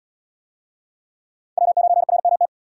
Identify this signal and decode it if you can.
Morse code, heard as a keyed tone, about 69 wpm, 700 Hz. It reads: W9RMT